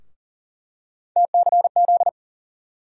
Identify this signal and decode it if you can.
Morse code, heard as a keyed tone, about 40 wpm, 700 Hz. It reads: TCZ
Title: USA Call Sign Postfixes - 3 Letters 40wpm